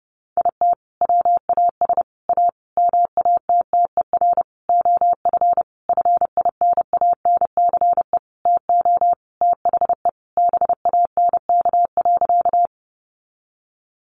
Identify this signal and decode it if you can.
Morse code, heard as a keyed tone, about 30 wpm, 700 Hz. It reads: IT WAS A MATTER OF FINANCE TO THE BANK.
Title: Sentences from Top 1500 Words 30wpm